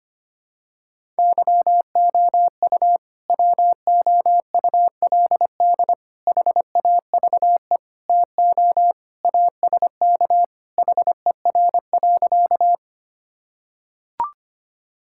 Morse code, heard as a keyed tone, about 25 wpm, 700 Hz. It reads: YOU WOULD HAVE TO ASK HER.
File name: Sentences with 3-Word Phrases - 25wpm